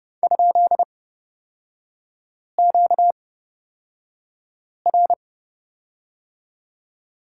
Morse code, heard as a keyed tone, about 29 wpm, spaced as for 5 wpm, 700 Hz. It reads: ?QR
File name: Lesson 38e - Letter q - Three Characters 30wpm